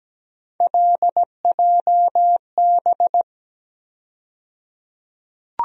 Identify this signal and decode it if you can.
Morse code, heard as a keyed tone, about 17 wpm, 700 Hz. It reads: LJB
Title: USA Call Sign Postfixes - 3 Letters 17wpm